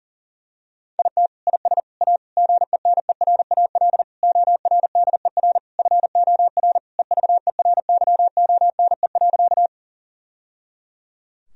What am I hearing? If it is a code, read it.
Morse code, heard as a keyed tone, about 37 wpm, 700 Hz. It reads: IT IS A GENERAL ORDER FOR EVERYONE.
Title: Sentences from Top 1000 Words - Review Part 3 - 40wpm